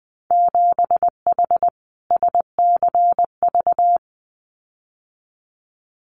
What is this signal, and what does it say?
Morse code, heard as a keyed tone, about 20 wpm, 700 Hz. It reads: 7H SCV
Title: Winter Field Day Exchanges Encore 20wpm